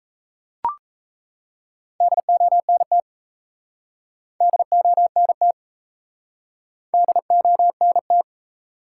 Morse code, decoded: DONT DONT DONT

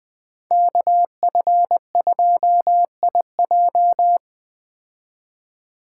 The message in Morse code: KF2IJ

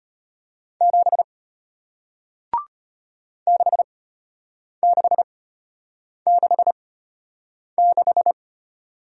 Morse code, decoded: 7 6666

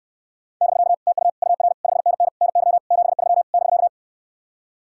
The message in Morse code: <BK> TU UR 5NN NY 73 <BK>

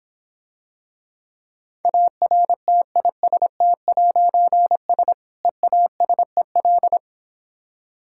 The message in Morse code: ARTIST'S EASEL

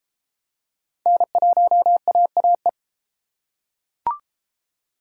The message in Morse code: N1AAE

